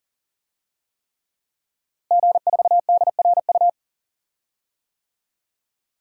Morse code, heard as a keyed tone, about 40 wpm, 700 Hz. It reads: G4DRU
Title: DXCC-100 Call Signs - 5 Characters - Repeated 3x - 40wpm